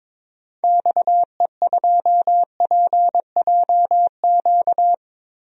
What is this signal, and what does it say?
Morse code, heard as a keyed tone, about 21 wpm, 700 Hz. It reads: XE2PJQ
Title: DXCC-100 Call Signs - 6 Characters - 22wpm